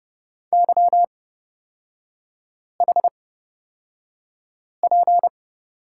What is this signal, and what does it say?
Morse code, heard as a keyed tone, about 30 wpm, spaced as for 5 wpm, 700 Hz. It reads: YHP
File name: Lesson 36e - BT - Three Characters 30wpm